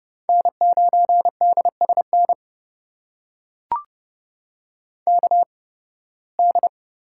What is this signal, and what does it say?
Morse code, heard as a keyed tone, about 31 wpm, spaced as for 20 wpm, 700 Hz. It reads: N9DSN KD
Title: USA Call Signs - 5 Characters - 8x Character Spacing - 30wpm